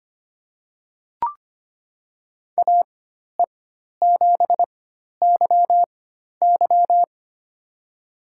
Morse code, heard as a keyed tone, about 25 wpm, spaced as for 12 wpm, 700 Hz. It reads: AE7YY
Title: USA Call Signs - 5 Characters - 4x Character Spacing - 25wpm